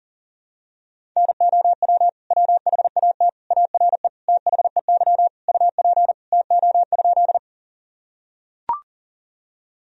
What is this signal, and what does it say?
Morse code, heard as a keyed tone, about 40 wpm, 700 Hz. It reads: NOW WHAT ARE THEY UP TO?
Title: Sentences from Top 1000 Words - Review Part 1 - 40wpm